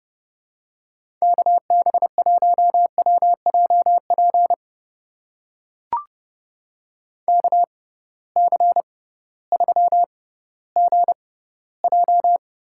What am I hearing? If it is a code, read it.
Morse code, heard as a keyed tone, about 30 wpm, spaced as for 16 wpm, 700 Hz. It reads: KB1WJP KC3GJ
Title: USA Call Signs - 6 Characters - 6x Character Spacing - 30wpm